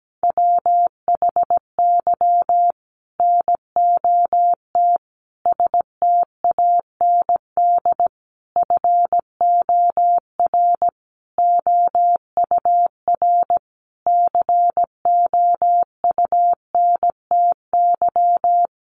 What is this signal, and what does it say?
Morse code, heard as a keyed tone, about 17 wpm, 700 Hz. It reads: WHY NOT STAND FOR OUR COUNTY
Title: Sentences from Top 1500 Words 17wpm